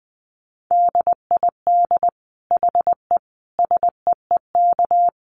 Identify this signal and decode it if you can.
Morse code, heard as a keyed tone, about 20 wpm, 700 Hz. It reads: DID HE SEEK
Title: Dit Challenge 20wpm